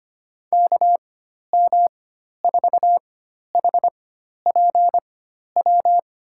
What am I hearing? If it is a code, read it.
Morse code, heard as a keyed tone, about 25 wpm, spaced as for 12 wpm, 700 Hz. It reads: KM4HPW
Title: USA Call Signs - 6 Characters - 4x Character Spacing - 25wpm